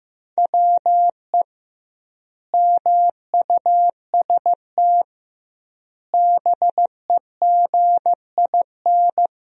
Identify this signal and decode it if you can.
Morse code, heard as a keyed tone, about 15 wpm, 700 Hz. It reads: WE MUST BEGIN
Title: 2x Word-Spacing - Sentences from Top 300 Words - 15wpm